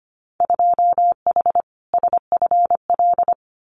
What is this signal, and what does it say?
Morse code, heard as a keyed tone, about 25 wpm, 700 Hz. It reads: 2H SFL
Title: Winter Field Day Exchanges Encore 25wpm